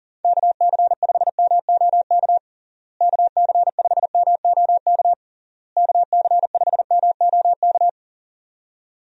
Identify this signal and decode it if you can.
Morse code, heard as a keyed tone, about 40 wpm, 700 Hz. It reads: KC5MOK KC5MOK KC5MOK
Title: USA Call Signs - 6 Characters - Repeated 3x - 40wpm